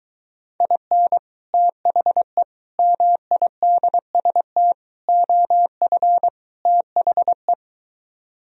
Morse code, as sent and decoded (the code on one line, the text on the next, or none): .. -. / - .... / -- .. -.. ... - / --- ..-. / - .... .
IN THE MIDST OF THE